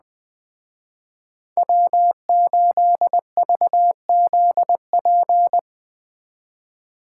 Morse code, heard as a keyed tone, about 20 wpm, 700 Hz. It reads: W8VZP